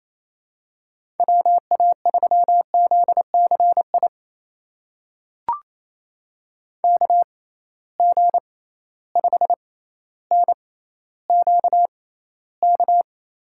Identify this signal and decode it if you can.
Morse code, heard as a keyed tone, about 28 wpm, spaced as for 14 wpm, 700 Hz. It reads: WA3ZCI KG5NQK
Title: USA Call Signs - 6 Characters - 6x Character Spacing - 28wpm